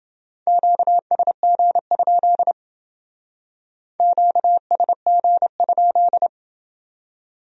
Morse code, heard as a keyed tone, about 30 wpm, 700 Hz. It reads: QSG? QSG?